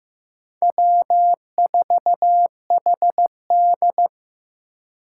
Morse code, heard as a keyed tone, about 15 wpm, 700 Hz. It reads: W4HD
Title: Four Challenging Letter-Number Combinations 15wpm